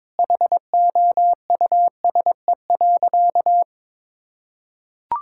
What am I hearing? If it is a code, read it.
Morse code, heard as a keyed tone, about 22 wpm, 700 Hz. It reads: HOUSE.
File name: practice / 22wpm / Sentences with 4-Word Phrases - 22wpm